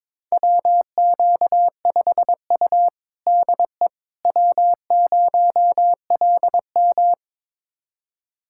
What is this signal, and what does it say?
Morse code, heard as a keyed tone, about 22 wpm, 700 Hz. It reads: WQ5U DE W0LM
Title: USA Callsign DE Callsign - 4 Characters - 22wpm